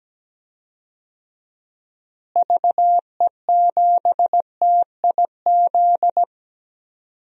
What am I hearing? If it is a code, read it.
Morse code, heard as a keyed tone, about 17 wpm, 700 Hz. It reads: VE7TIZ
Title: DXCC-100 Call Signs - 6 Characters - Repeated 2x - 17wpm